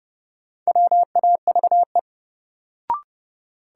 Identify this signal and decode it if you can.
Morse code, heard as a keyed tone, about 30 wpm, 700 Hz. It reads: WAVE
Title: Sentences from Top 1400 Words - New Word Drill Alt - 30wpm